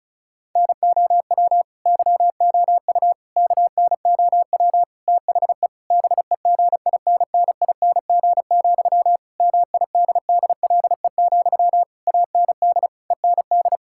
Morse code, heard as a keyed tone, about 34 wpm, 700 Hz. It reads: NOW YOU KNOW THE BEGINNING, MIDDLE, AND END
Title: Sentences from Top 1000 Words - Review Part 6 - 35wpm